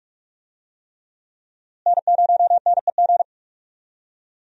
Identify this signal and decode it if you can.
Morse code, heard as a keyed tone, about 44 wpm, 700 Hz. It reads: N0NEG
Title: USA Call Signs - 5 Characters - Repeated 3x - 45wpm